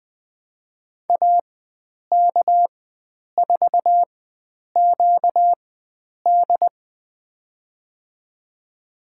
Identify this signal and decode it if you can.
Morse code, heard as a keyed tone, about 20 wpm, spaced as for 9 wpm, 700 Hz. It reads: AK4QD
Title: USA Call Signs - 5 Characters - 4x Character Spacing - 20wpm